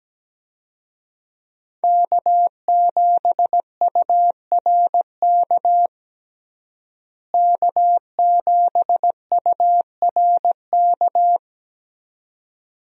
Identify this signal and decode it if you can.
Morse code, heard as a keyed tone, about 17 wpm, 700 Hz. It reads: K7URK K7URK
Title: USA Call Signs - 5 Characters - Repeated 2x - 17wpm